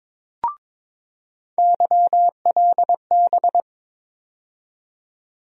Morse code, heard as a keyed tone, about 22 wpm, 700 Hz. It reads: YLB
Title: USA Call Sign Postfixes - 3 Letters 22wpm